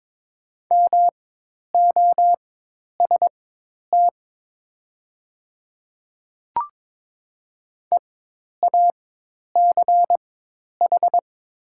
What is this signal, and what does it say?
Morse code, heard as a keyed tone, about 22 wpm, spaced as for 10 wpm, 700 Hz. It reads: MOST EACH
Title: 4x Character Spacing - Top 100 Words - 22wpm